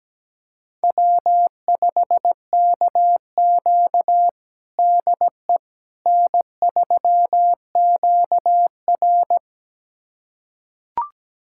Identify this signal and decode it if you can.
Morse code, heard as a keyed tone, about 17 wpm, 700 Hz. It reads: W5KQ DE N3QR
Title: USA Callsign DE Callsign - 4 Characters - 17wpm